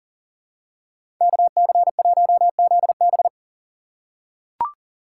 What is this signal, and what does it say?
Morse code, heard as a keyed tone, about 40 wpm, 700 Hz. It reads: KC1ZB
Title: USA Call Signs - 5 Characters - 6x Character Spacing - 40wpm